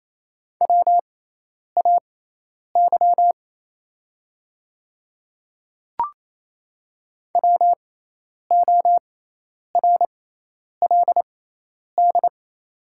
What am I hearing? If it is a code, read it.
Morse code, heard as a keyed tone, about 28 wpm, spaced as for 10 wpm, 700 Hz. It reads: WAY WORLD
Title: 6x Character Spacing - Top 200 Words - 28wpm